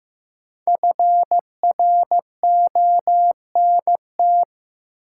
Morse code, heard as a keyed tone, about 15 wpm, 700 Hz. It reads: FRONT
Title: Sentences from Top 500 Words - New Word Drill - 15wpm